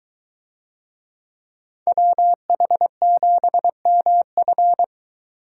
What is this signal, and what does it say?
Morse code, heard as a keyed tone, about 23 wpm, 700 Hz. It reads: WH7MF